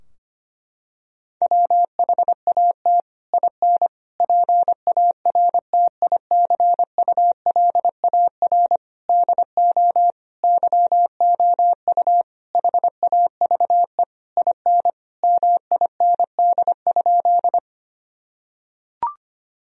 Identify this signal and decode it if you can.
Morse code, heard as a keyed tone, about 25 wpm, 700 Hz. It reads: WHAT IN PARTICULAR DO YOU HAVE IN MIND?
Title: Sentences from Top 1000 Words - Review Part 4 - 25wpm